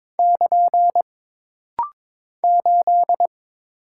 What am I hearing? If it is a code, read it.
Morse code, heard as a keyed tone, about 22 wpm, 700 Hz. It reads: <KN> 8
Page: This practice set is composed of Letters, Numbers, common punctuations, and common prosigns.